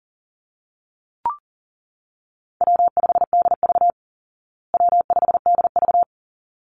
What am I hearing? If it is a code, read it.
Morse code, heard as a keyed tone, about 40 wpm, 700 Hz. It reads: W5DV W5DV
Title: Call Signs Repeated 3x - 40wpm